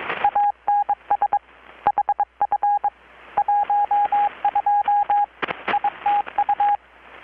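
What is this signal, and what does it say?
Morse code, heard as a keyed tone, about 22 wpm, 820 Hz, with radio noise. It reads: ANS HF 12 ITU
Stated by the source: Morse from KPH's Night of Nights XX maritime radiotelegraphy event, received in Chiapas, Mexico about 3,800 km away on a Kenwood R-600 with a grounded 10 m long-wire antenna.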